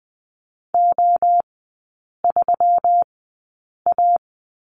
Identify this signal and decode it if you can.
Morse code, heard as a keyed tone, about 20 wpm, spaced as for 8 wpm, 700 Hz. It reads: O3A
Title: 121 Lesson 20e - Number 3 - Three Characters 20wpm